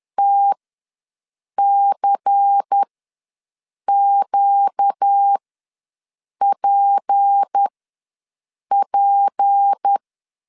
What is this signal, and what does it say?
Morse code, heard as a keyed tone, about 10 wpm, spaced as for 6 wpm, 800 Hz. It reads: TCQPP
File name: LEZIONI / 6 PKX0 / 6 P ws 10wpm 3751